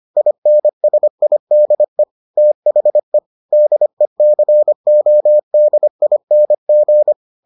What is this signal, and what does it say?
Morse code, heard as a keyed tone, about 25 wpm, 600 Hz. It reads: INSIDE THE DECODING